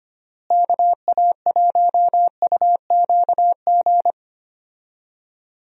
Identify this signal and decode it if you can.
Morse code, heard as a keyed tone, about 25 wpm, 700 Hz. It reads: KA1UQG
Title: USA Call Signs - 6 Characters - Repeated 3x - 25wpm